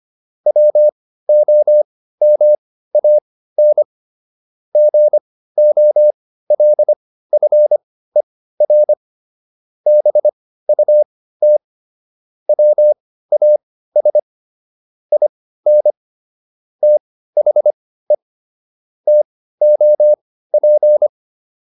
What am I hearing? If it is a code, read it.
Morse code, heard as a keyed tone, about 25 wpm, spaced as for 15 wpm, 600 Hz. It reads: WOMAN GOLFER BUT WAS IN THE TOP